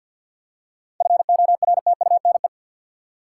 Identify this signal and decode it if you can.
Morse code, heard as a keyed tone, about 47 wpm, 700 Hz. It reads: FORTUNE
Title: Seven Letter Words - 50wpm